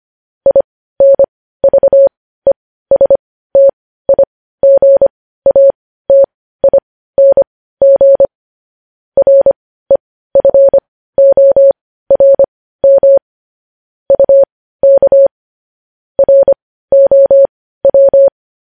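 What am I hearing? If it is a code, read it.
Morse code, heard as a keyed tone, about 25 wpm, spaced as for 15 wpm, 550 Hz. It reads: INVESTIGATING REFORM UK ROW
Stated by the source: News headlines in Morse code at 15 words per minute, characters sent at 25 WPM.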